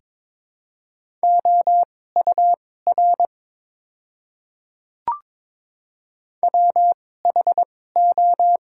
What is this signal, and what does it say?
Morse code, heard as a keyed tone, about 22 wpm, spaced as for 16 wpm, 700 Hz. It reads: OUR WHO
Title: 2x Character Spacing - Top 200 Words - 22wpm